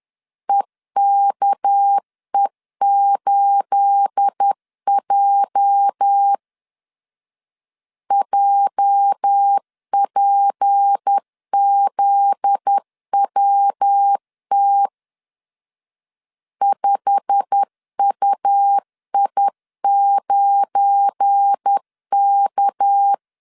Morse code, heard as a keyed tone, about 10 wpm, 800 Hz. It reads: EKE8J JPZWT 5UI9K
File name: CW / DET / 1 9 1500C S A 10wpm 4511